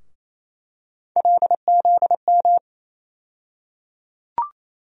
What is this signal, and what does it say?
Morse code, heard as a keyed tone, about 28 wpm, 700 Hz. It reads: LZM
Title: USA Call Sign Postfixes - 3 Letters 28wpm